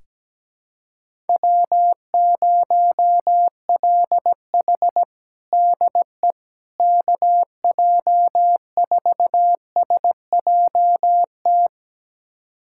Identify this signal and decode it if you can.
Morse code, heard as a keyed tone, about 17 wpm, 700 Hz. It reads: W0LH DE KJ4SJT
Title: USA Callsign DE Callsign 17wpm